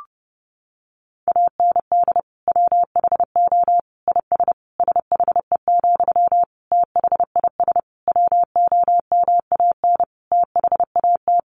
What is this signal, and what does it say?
Morse code, heard as a keyed tone, about 30 wpm, 700 Hz. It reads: AND WHO IS SHE, THIS WOMAN THAT
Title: Sentences from Top 1100 Words 30wpm